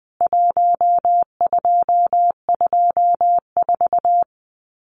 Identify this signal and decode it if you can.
Morse code, heard as a keyed tone, about 20 wpm, 700 Hz. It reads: 1224
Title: CWT Exchange 20wpm